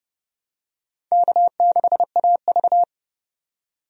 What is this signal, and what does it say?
Morse code, heard as a keyed tone, about 29 wpm, 700 Hz. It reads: K6AV
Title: USA Call Signs - 4 Characters - 30wpm